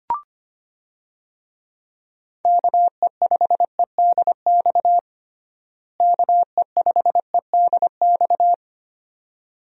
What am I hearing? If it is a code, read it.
Morse code, heard as a keyed tone, about 25 wpm, 700 Hz. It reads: KE5EDX KE5EDX
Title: USA Call Signs - 6 Characters - Repeated 2x - 25wpm